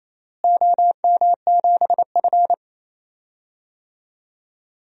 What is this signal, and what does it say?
Morse code, heard as a keyed tone, about 28 wpm, 700 Hz. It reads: OM7F
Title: DXCC-100 Call Signs - 4 Characters - Repeated 2x - 28wpm